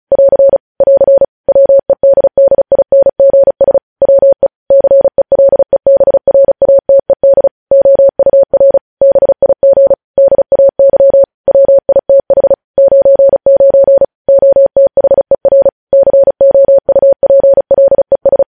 Morse code, heard as a keyed tone, about 36 wpm, 550 Hz. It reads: <AR> <AR> WEDDINGS WE CELEBRATED OUR BIG DAY WITH 99 OTHER COUPLES